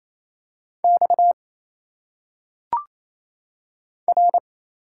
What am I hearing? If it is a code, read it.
Morse code, heard as a keyed tone, about 28 wpm, 700 Hz. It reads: X R
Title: Single Letters - Phonetic Pronunciation 28wpm